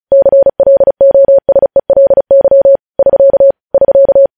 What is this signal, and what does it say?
Morse code, heard as a keyed tone, about 36 wpm, 550 Hz. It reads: CLOSELY <SK> <SK>
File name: podcast / News headlines-1x-35x35